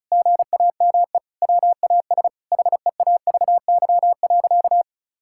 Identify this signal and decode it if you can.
Morse code, heard as a keyed tone, about 36 wpm, 690 Hz. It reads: GAME WAS HEAVY.
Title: Sentences from Top 1600 Words 35wpm